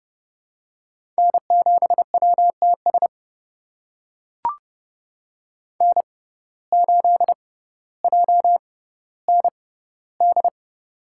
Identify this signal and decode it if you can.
Morse code, heard as a keyed tone, about 30 wpm, spaced as for 15 wpm, 700 Hz. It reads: N7WTS N8JND
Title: USA Call Signs - 5 Characters - 6x Character Spacing - 30wpm